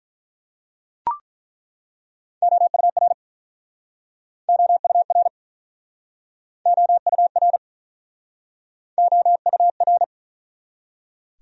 Morse code, heard as a keyed tone, about 49 wpm, 700 Hz. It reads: OUR OUR OUR OUR